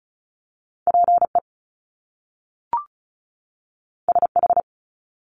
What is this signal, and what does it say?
Morse code, heard as a keyed tone, about 36 wpm, 700 Hz. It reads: PE SH
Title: Top Two Letter Combinations in Words 35wpm